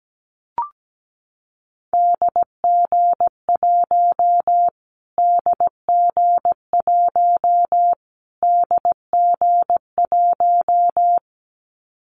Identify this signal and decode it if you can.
Morse code, heard as a keyed tone, about 17 wpm, 700 Hz. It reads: DG1 DG1 DG1
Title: Callsign Prefixes for Top 25 DXCC - Repeated 3x - 17wpm